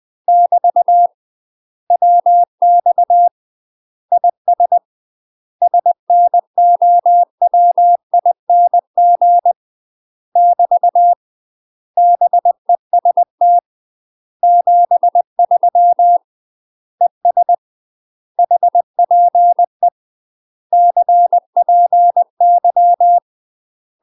Morse code, heard as a keyed tone, about 20 wpm, 700 Hz. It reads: <BT> WX IS SNOWING <BT> BEST 73 ES HPE CPY
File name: UK MASTER QSOs TRAINING/20WPM